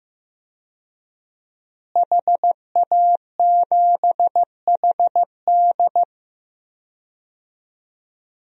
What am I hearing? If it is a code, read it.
Morse code, heard as a keyed tone, about 15 wpm, 700 Hz. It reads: HA7HD
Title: DXCC-100 Call Signs - 5 Characters - Repeated 3x - 15wpm